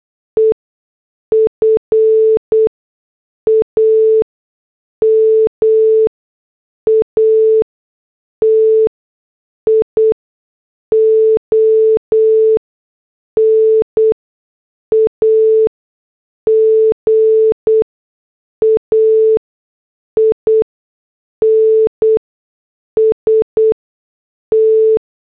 Morse code, heard as a keyed tone, about 8 wpm, spaced as for 6 wpm, 440 Hz. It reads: EFAMATION AGAINST
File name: morse / headlines